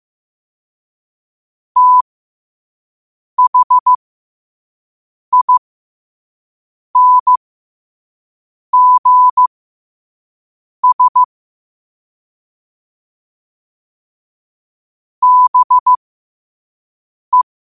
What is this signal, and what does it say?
Morse code, heard as a keyed tone, about 15 wpm, spaced as for 5 wpm, 1000 Hz. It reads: THINGS BE
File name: 5wpm